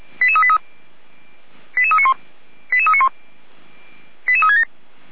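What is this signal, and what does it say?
Five-tone selective call (ZVEI), heard as quick runs of five tones, tones of about 70 ms. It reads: 88383 88391 88391 88367
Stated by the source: ZVEI five-tone selective call code 88335, received by SDR, from fire brigade Feuerwehr Orlamünde's audio server.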